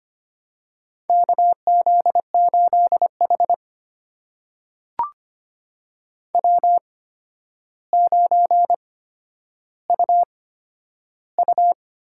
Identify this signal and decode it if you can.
Morse code, heard as a keyed tone, about 25 wpm, spaced as for 11 wpm, 700 Hz. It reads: KZ8H W9UU
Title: USA Call Signs - 4 Characters - 8x Character Spacing - 25wpm